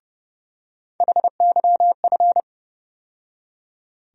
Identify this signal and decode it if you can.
Morse code, heard as a keyed tone, about 30 wpm, 700 Hz. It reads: HYF
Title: USA Call Sign Postfixes - 3 Letters 30wpm